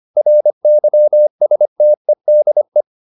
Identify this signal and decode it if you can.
Morse code, heard as a keyed tone, about 25 wpm, 600 Hz. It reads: RYSTEDE
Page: Ferie 25 wpm